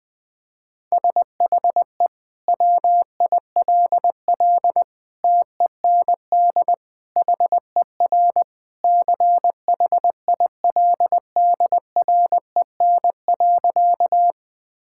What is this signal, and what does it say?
Morse code, heard as a keyed tone, about 21 wpm, 700 Hz. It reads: SHE WILL TEND HER CHILDREN.
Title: Sentences from Top 1000 Words - Review Part 5 - 20wpm